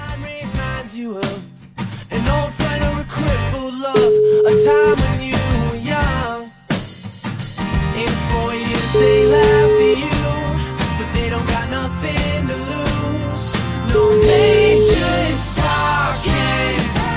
Ringback tone (Europe, 425 Hz), with noise.